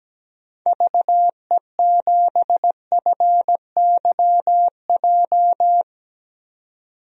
Morse code, heard as a keyed tone, about 17 wpm, 700 Hz. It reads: VE7FYJ
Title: DXCC-100 Call Signs - 6 Characters - 17wpm